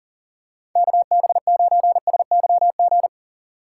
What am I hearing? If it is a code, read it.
Morse code, heard as a keyed tone, about 40 wpm, 700 Hz. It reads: KB9SYG
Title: USA Call Signs - 2x Character Spacing - 40wpm